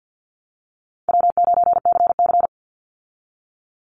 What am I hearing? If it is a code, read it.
Morse code, heard as a keyed tone, about 48 wpm, 700 Hz. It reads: W9CC
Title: Call Signs Repeated 2x - 50wpm